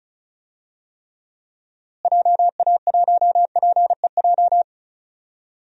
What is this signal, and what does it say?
Morse code, heard as a keyed tone, about 36 wpm, 700 Hz. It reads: JA1PEJ
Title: DXCC-100 Call Signs - 6 Characters - Repeated 3x - 35wpm